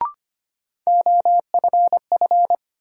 Morse code, heard as a keyed tone, about 25 wpm, 700 Hz. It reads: OFF
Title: Sentences from Top 200 Words - New Word Drill - 25wpm